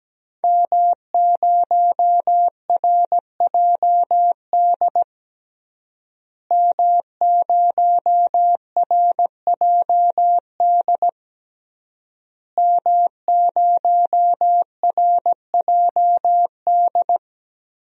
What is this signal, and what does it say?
Morse code, heard as a keyed tone, about 17 wpm, 700 Hz. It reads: M0RJD M0RJD M0RJD
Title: DXCC-100 Call Signs - 5 Characters - Repeated 3x - 17wpm